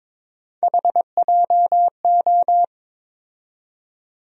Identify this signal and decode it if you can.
Morse code, heard as a keyed tone, about 22 wpm, 700 Hz. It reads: HJO
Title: USA Call Sign Postfixes - 3 Letters 22wpm